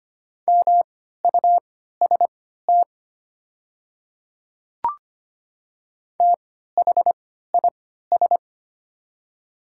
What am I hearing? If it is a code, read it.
Morse code, heard as a keyed tone, about 25 wpm, spaced as for 14 wpm, 700 Hz. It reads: MUST THIS
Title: 3x Character Spacing - Top 200 Words - 25wpm